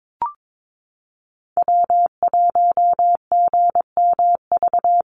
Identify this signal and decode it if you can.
Morse code, heard as a keyed tone, about 22 wpm, 690 Hz. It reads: W1GMV
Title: USA Call Signs Repeated 3x 22wpm